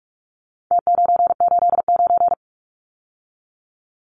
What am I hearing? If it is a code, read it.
Morse code, heard as a keyed tone, about 44 wpm, 700 Hz. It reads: T989
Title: Numbers - Contest Style - 4-Digit 45wpm